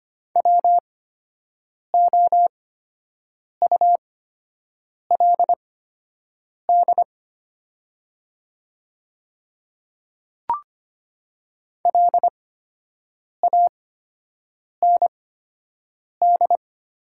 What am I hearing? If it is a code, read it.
Morse code, heard as a keyed tone, about 25 wpm, spaced as for 7 wpm, 700 Hz. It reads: WOULD LAND
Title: 8x Character Spacing - Top 500 Words - 25wpm